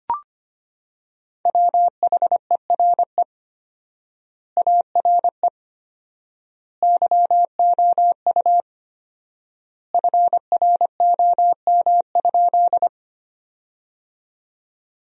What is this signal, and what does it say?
Morse code, heard as a keyed tone, about 25 wpm, 700 Hz. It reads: WHERE ARE YOU FROM?
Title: 4x Word-Spacing - Sentences from Top 200 Words - 25wpm